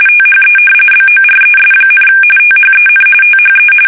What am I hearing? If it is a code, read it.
RTTY, about 45.45 baud. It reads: THE BORES, THEN THE
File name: fsk850 / battleshipnj